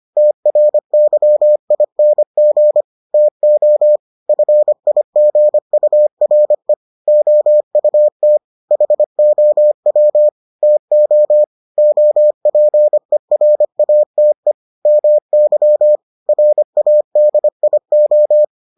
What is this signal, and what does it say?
Morse code, heard as a keyed tone, about 25 wpm, 600 Hz. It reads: TRYING TO FIGURE OUT HOW TO OPERATE MY RADIO